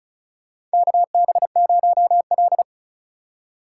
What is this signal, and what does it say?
Morse code, heard as a keyed tone, about 34 wpm, 700 Hz. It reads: KB0L
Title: USA Call Signs - 4 Characters - 4x Character Spacing - 35wpm